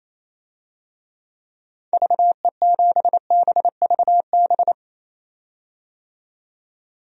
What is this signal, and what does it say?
Morse code, heard as a keyed tone, about 28 wpm, 700 Hz. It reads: VE7BVB
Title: DXCC-100 Call Signs - 6 Characters - 28wpm